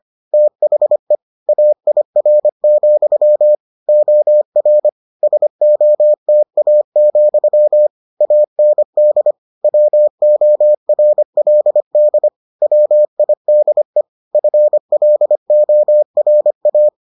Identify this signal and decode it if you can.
Morse code, heard as a keyed tone, about 25 wpm, 600 Hz. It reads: THE AIR, OR SOTA, AND WORLD WIDE FLORA